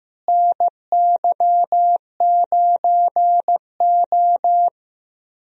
Morse code, heard as a keyed tone, about 15 wpm, 700 Hz. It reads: NY9O
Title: USA Call Signs - 4 Characters - Repeated 3x - 15wpm